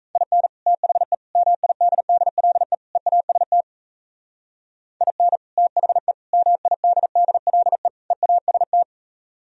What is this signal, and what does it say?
Morse code, heard as a keyed tone, about 35 wpm, 700 Hz. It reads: IN THE MIDDLE EAST IN THE MIDDLE EAST